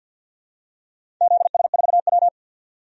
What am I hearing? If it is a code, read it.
Morse code, heard as a keyed tone, about 51 wpm, 700 Hz. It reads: ZS4W
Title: DXCC-100 Call Signs - 4 Characters - 50wpm